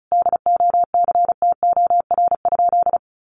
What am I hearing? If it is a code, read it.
Morse code, heard as a keyed tone, about 34 wpm, 700 Hz. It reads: DOCTOR?
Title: Sentences from Top 1500 Words 35wpm